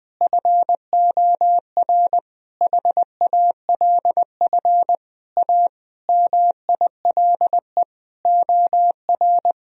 Morse code, heard as a keyed tone, about 21 wpm, 700 Hz. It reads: FOR HALF A MILE OR
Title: Sentences from Top 1000 Words - Review Part 1 - 20wpm